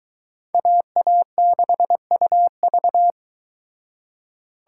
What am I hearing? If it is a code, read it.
Morse code, heard as a keyed tone, about 23 wpm, 700 Hz. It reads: AA6UV